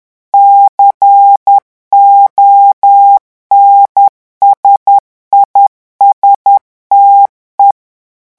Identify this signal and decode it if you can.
Morse code, heard as a keyed tone, about 10 wpm, 800 Hz. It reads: CONSISTE
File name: revision9 citations humoristiques vitesse 10 mots minute